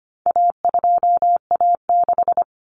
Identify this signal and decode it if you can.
Morse code, heard as a keyed tone, about 25 wpm, 700 Hz. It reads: A2A6
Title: Common Cut Numbers - 4-Digit 25wpm